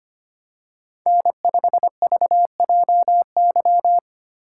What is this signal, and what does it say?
Morse code, heard as a keyed tone, about 25 wpm, 700 Hz. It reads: N5VJY
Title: USA Call Signs - 6x Character Spacing - 25wpm